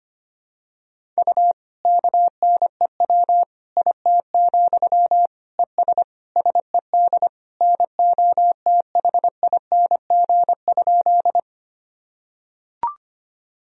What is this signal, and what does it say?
Morse code, heard as a keyed tone, about 25 wpm, 700 Hz. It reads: U KNEW IT, ES SED NOTHING?
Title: Sentences from Top 500 Words and Abbreviations 25wpm